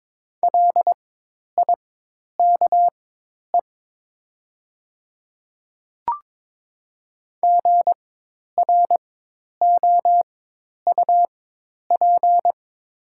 Morse code, heard as a keyed tone, about 22 wpm, spaced as for 10 wpm, 700 Hz. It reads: LIKE GROUP
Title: 4x Character Spacing - Top 300 Words - 22wpm